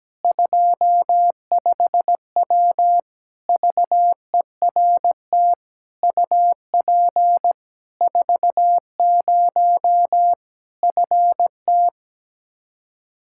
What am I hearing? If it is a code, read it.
Morse code, heard as a keyed tone, about 17 wpm, 700 Hz. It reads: 25W VERT UP 40 FT
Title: K1USN SST QSO Practice 17wpm